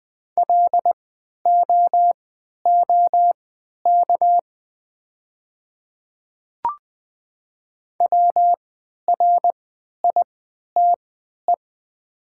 Morse code, heard as a keyed tone, about 20 wpm, spaced as for 11 wpm, 700 Hz. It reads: LOOK WRITE
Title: 3x Character Spacing - Top 100 Words - 20wpm